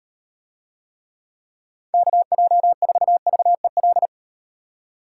Morse code, heard as a keyed tone, about 38 wpm, 700 Hz. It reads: KJ4VEL